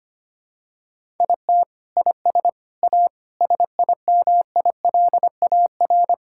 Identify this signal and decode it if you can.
Morse code, heard as a keyed tone, about 25 wpm, 700 Hz. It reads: IT IS A SIMILAR
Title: Sentences from Top 1000 Words - Review Part 5 - 25wpm